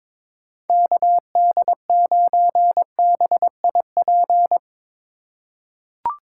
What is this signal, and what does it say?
Morse code, heard as a keyed tone, about 22 wpm, 700 Hz. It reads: KD9BIP
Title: USA Call Signs - 6 Characters - 3x Character Spacing - 22wpm